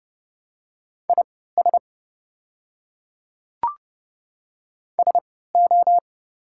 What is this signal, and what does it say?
Morse code, heard as a keyed tone, about 30 wpm, spaced as for 17 wpm, 690 Hz. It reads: IS SO
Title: 3x Character Spacing - Top 300 Words - 30wpm